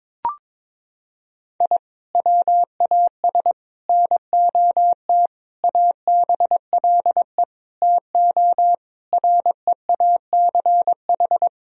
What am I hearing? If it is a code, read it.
Morse code, heard as a keyed tone, about 22 wpm, 700 Hz. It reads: I WAS NOT ABLE TO REACH
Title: Sentences from Top 1000 Words - Review Part 2 - 22wpm